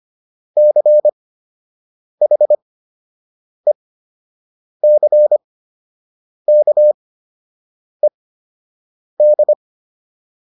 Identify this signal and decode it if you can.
Morse code, heard as a keyed tone, about 25 wpm, spaced as for 7 wpm, 600 Hz. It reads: CHECKED